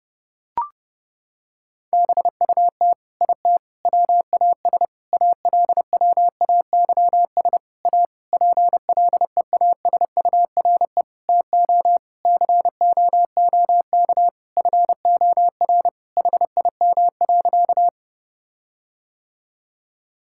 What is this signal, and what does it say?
Morse code, heard as a keyed tone, about 30 wpm, 700 Hz. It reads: BUT IT WAS ALWAYS A PLEASURE TO COOK FOR HIM.
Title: Sentences from Top 1600 Words 30wpm